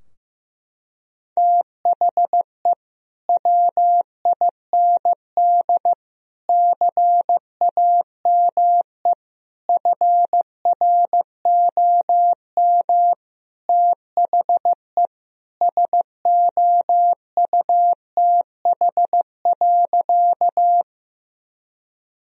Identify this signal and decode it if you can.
Morse code, heard as a keyed tone, about 15 wpm, 700 Hz. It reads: THE WIND CAME FROM THE SOUTH.
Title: Sentences from Top 1000 Words - Review Part 2 - 15wpm